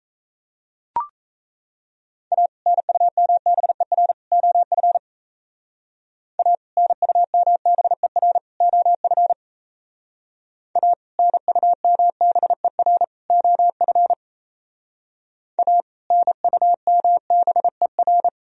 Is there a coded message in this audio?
Yes: Morse code from 2.3 to 18.3 s, heard as a keyed tone, about 36 wpm, 700 Hz. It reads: A NUMBER OF A NUMBER OF A NUMBER OF A NUMBER